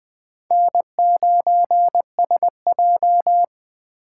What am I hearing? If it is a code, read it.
Morse code, heard as a keyed tone, about 20 wpm, 700 Hz. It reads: N9SJ
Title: USA Call Signs - 4 Characters - 4x Character Spacing - 20wpm